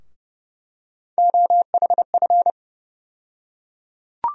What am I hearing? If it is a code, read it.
Morse code, heard as a keyed tone, about 30 wpm, 700 Hz. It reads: OHF
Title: USA Call Sign Postfixes - 3 Letters 30wpm